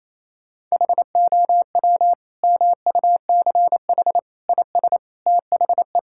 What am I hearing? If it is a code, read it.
Morse code, heard as a keyed tone, about 27 wpm, 700 Hz. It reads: HOW MUCH IS THE
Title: Sentences from Top 1000 Words - Review Part 6 - 28wpm